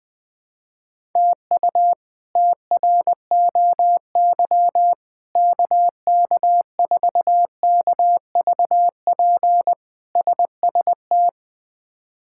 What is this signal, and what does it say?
Morse code, heard as a keyed tone, about 20 wpm, 700 Hz. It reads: TU TROY KK4KVP SST
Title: K1USN SST QSO Practice - 20wpm